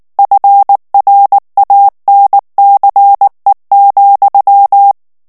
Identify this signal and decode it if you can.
Morse code, heard as a keyed tone, about 19 wpm, 800 Hz. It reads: FRANCE,
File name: morse mp3 / FRANCAIS TEXTE CLAIR 1 VITESSE 18 MOTS MINUTES